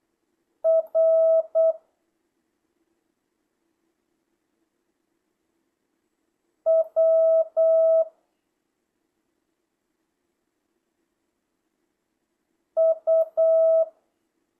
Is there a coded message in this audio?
Yes: Morse code from 0.7 to 13.8 s, heard as a keyed tone, about 8 wpm, 650 Hz. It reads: R W U